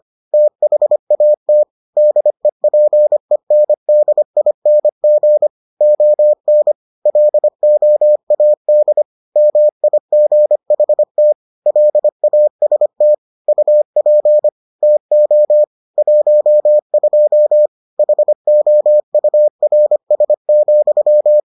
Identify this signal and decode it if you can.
Morse code, heard as a keyed tone, about 25 wpm, 600 Hz. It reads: THAT DEPENDING ON LOAD MIGHT LAST UP TO 12 HOURS,